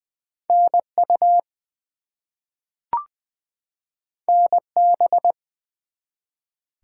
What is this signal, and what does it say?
Morse code, heard as a keyed tone, about 20 wpm, 700 Hz. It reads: NU NB
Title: US States and Canadian Provinces 20wpm